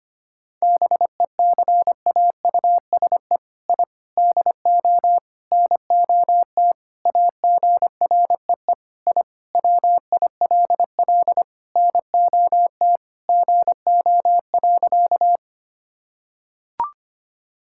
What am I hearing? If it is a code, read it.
Morse code, heard as a keyed tone, about 25 wpm, 700 Hz. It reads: BECAUSE I DO NOT AGREE I WILL NOT GO.
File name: practice / 25wpm / Sentences with 4-Word Phrases - 25wpm